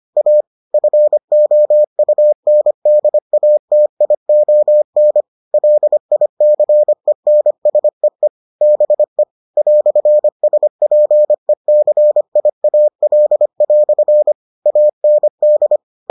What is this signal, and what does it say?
Morse code, heard as a keyed tone, about 25 wpm, 600 Hz. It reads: A FOUNDATION LICENSEE BE "SPECIAL" AND